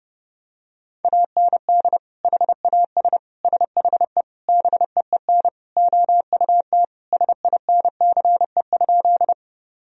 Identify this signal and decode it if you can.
Morse code, heard as a keyed tone, about 30 wpm, 700 Hz. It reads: AND HAS SHE BEEN OUT SINCE?
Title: Sentences from Top 1000 Words - Review Part 1 - 30wpm